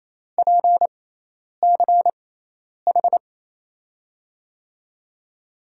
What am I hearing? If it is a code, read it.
Morse code, heard as a keyed tone, about 28 wpm, spaced as for 10 wpm, 700 Hz. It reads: PCH
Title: USA Call Signs - 6 Characters - 6x Character Spacing - 28wpm